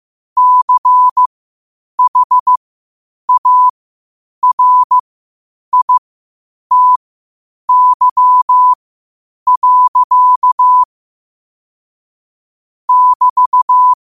Morse code, heard as a keyed tone, about 15 wpm, spaced as for 8 wpm, 1000 Hz. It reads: CHARITY. <BT>
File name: qotdfiles / 8wpm